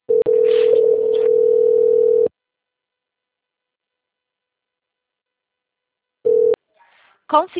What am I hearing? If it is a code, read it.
Ringback tone (North America), 440 + 480 Hz.